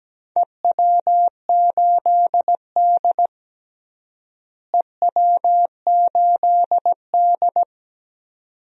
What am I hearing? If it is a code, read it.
Morse code, heard as a keyed tone, about 17 wpm, 700 Hz. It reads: EW8D EW8D
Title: DXCC-100 Call Signs - 4 Characters - Repeated 2x - 17wpm